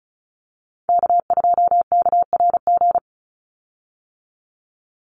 Morse code, heard as a keyed tone, about 34 wpm, 700 Hz. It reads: K2KRG
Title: Call Signs - 35wpm